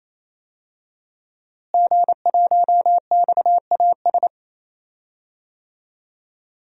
Morse code, heard as a keyed tone, about 28 wpm, 700 Hz. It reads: G1XAS
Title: DXCC-100 Call Signs - 5 Characters - Repeated 3x - 28wpm